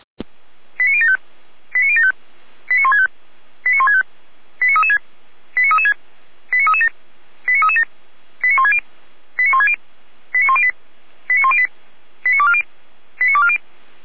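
Five-tone selective call (ZVEI), heard as quick runs of five tones, tones of about 70 ms. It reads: 89075 89075 89165 89165 89336 89336 89337 89337 89160 89160 89108 89108 89240 89240
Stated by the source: ZVEI five-tone selective call received on an SDR.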